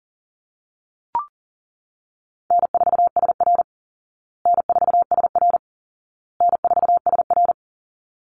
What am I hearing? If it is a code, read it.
Morse code, heard as a keyed tone, about 40 wpm, 700 Hz. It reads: N4SR N4SR N4SR
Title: Easy Call Signs Repeated 3x - 40wpm